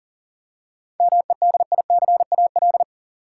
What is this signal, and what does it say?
Morse code, heard as a keyed tone, about 40 wpm, 700 Hz. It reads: MEDICAL
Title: Seven Letter Words - 40wpm